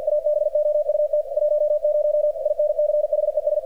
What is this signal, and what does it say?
Morse code, heard as a keyed tone, about 48 wpm, 600 Hz, with radio noise. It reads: ABOUT 10 INCHES
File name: archived / CRSnet